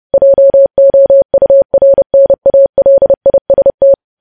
Morse code, heard as a keyed tone, about 30 wpm, 550 Hz. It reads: JOURNALIST
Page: news headlines in morse code at 30 wpm